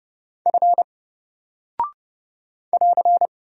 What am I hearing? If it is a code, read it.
Morse code, heard as a keyed tone, about 30 wpm, 700 Hz. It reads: F <AR>
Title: Single Character (Letter:Number:Punctuation:Prosign) 30wpm